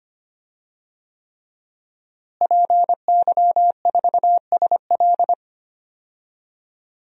Morse code, heard as a keyed tone, about 25 wpm, 700 Hz. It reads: PY4SL
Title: DXCC-100 Call Signs - 5 Characters - Repeated 3x - 25wpm